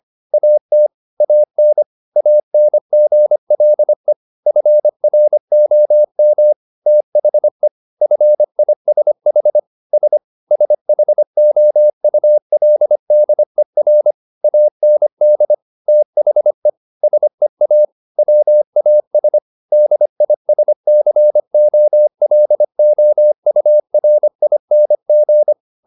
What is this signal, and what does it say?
Morse code, heard as a keyed tone, about 25 wpm, 600 Hz. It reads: AT AN ANGLE FROM THE FISH S SHOULDER AND THE SEA WAS DISCOLOURING